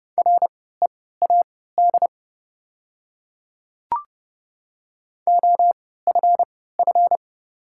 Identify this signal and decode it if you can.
Morse code, heard as a keyed tone, about 30 wpm, spaced as for 17 wpm, 700 Hz. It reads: READ OFF
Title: 3x Character Spacing - Top 300 Words - 30wpm